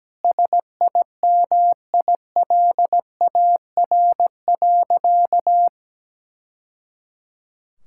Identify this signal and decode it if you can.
Morse code, heard as a keyed tone, about 17 wpm, 700 Hz. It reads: SIMILAR.
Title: Sentences from Top 1000 Words - Review Part 5 - 17wpm